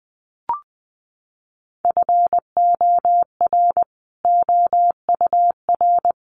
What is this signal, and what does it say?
Morse code, heard as a keyed tone, about 20 wpm, 700 Hz. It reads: FOR OUR
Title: Sentences from Top 1400 Words 20wpm